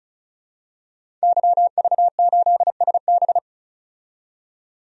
Morse code, heard as a keyed tone, about 34 wpm, 700 Hz. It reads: YV8SB
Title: DXCC-100 Call Signs - 5 Characters - 35wpm